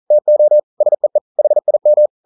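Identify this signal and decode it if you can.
Morse code, heard as a keyed tone, about 42 wpm, 600 Hz. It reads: TO SEE HIM
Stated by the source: Morse code practice at 41 wpm, from CW med Gnister.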